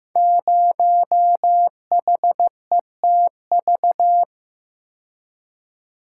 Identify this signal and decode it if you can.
Morse code, heard as a keyed tone, about 15 wpm, 700 Hz. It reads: 0HETV
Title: Code Groups - 5 Letter:Numbers - 15wpm